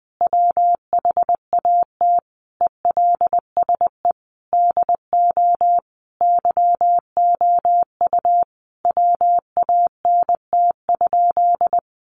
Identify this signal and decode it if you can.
Morse code, heard as a keyed tone, about 20 wpm, 700 Hz. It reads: WHAT ELSE DO YOU WANT?
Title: Sentences from Top 800 Words 20wpm